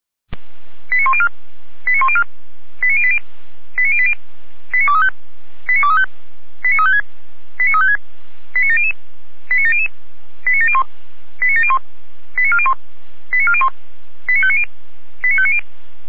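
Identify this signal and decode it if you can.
Five-tone selective call (ZVEI), heard as quick runs of five tones, tones of about 70 ms. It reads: 89104 89104 89080 89080 89235 89235 89356 89356 89700 89700 89701 89701 89401 89401 89590 89590
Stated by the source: ZVEI five-tone selective call received on SDR.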